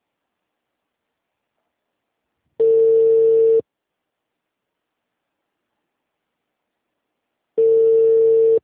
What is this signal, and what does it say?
Ringback tone (China, 450 Hz).
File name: out- -2016- - - .5307